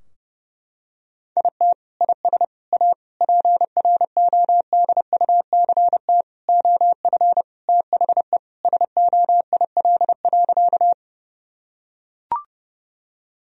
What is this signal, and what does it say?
Morse code, heard as a keyed tone, about 32 wpm, 700 Hz. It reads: IT IS A PRODUCT OF THE SOIL.
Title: Sentences from Top 1000 Words - Review Part 4 - 30wpm